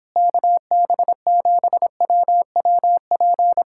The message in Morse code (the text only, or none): KB7WWP